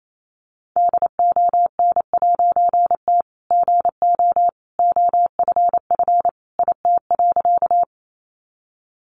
Morse code, heard as a keyed tone, about 28 wpm, 700 Hz. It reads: DON'T GO OFF IT.
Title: A Sound Of Thunder 28wpm